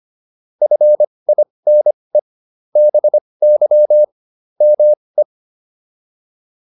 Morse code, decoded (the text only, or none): FINE BY ME